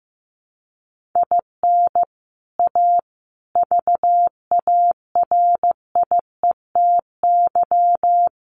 Morse code, heard as a keyed tone, about 15 wpm, 700 Hz. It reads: IN A VARIETY
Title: Sets of 4 Words 15wpm